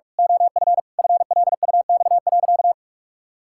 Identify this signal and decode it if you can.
Morse code, heard as a keyed tone, about 46 wpm, 700 Hz. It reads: OF FLUX.
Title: Pangram Challenge v2 - 45wpm